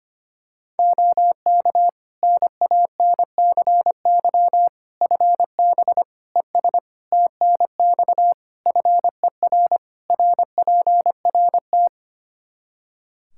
Morse code, heard as a keyed tone, about 25 wpm, 700 Hz. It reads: OK NANCY FB ES TNX FER RPRT
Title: Practice QSO - Exchange 2 - Info - 25wpm